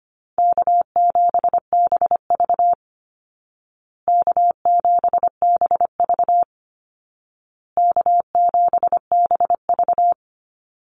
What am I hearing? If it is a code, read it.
Morse code, heard as a keyed tone, about 25 wpm, 700 Hz. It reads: K7BV K7BV K7BV
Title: Call Signs Repeated 2x - 25wpm